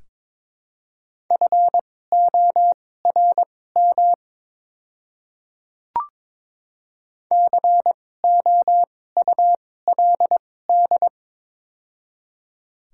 Morse code, heard as a keyed tone, about 22 wpm, spaced as for 16 wpm, 700 Hz. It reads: FORM COULD
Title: 2x Character Spacing - Top 300 Words - 22wpm